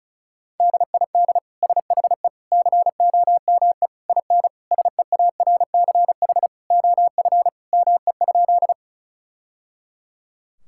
Morse code, heard as a keyed tone, about 36 wpm, 700 Hz. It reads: DID SHE COME IN SEARCH OF ME?
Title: Sentences from Top 1000 Words - Review Part 5 - 35wpm